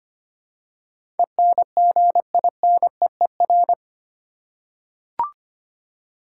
Morse code, heard as a keyed tone, about 25 wpm, 700 Hz. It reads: ENGINEER
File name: 8 Letter Words - First 3 Letters Given - 25wpm